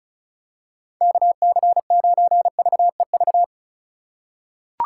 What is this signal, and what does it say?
Morse code, heard as a keyed tone, about 34 wpm, 700 Hz. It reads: KC9VEV
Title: USA Call Signs - 6 Characters - 4x Character Spacing - 35wpm